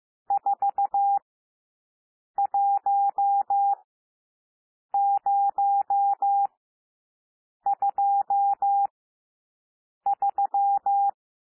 Morse code, heard as a keyed tone, about 15 wpm, spaced as for 6 wpm, 800 Hz. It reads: 41023
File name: SOUND NUMBERS 15 wpm